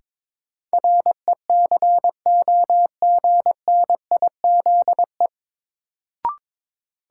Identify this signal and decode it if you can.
Morse code, heard as a keyed tone, about 22 wpm, 700 Hz. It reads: RECOGNIZE
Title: Sentences from Top 1400 Words - New Word Drill Alt - 22wpm